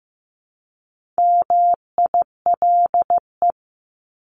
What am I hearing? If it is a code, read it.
Morse code, heard as a keyed tone, about 15 wpm, 700 Hz. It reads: MILE
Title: ES:S Suffix - Top 100 Words with Base Words 15wpm